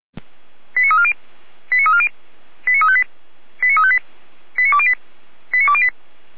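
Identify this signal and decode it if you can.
Five-tone selective call (ZVEI), heard as quick runs of five tones, tones of about 80 ms. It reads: 89340 89340 89358 89358 89208 89208
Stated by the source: ZVEI five-tone selective call received by SDR, from feuerwehr-orlamuende.org fire brigade audio archive.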